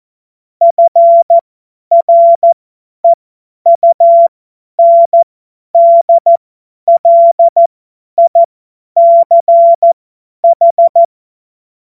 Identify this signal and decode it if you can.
Morse code, heard as a keyed tone, about 14 wpm, spaced as for 10 wpm, 680 Hz. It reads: FREUNDLICH
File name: Morsekurs / Witze / Witz 12 März 2024 50 F70